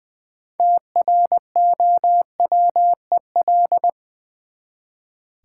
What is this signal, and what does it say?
Morse code, heard as a keyed tone, about 20 wpm, 700 Hz. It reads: TROWEL
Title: Easy Words with Hints - 20wpm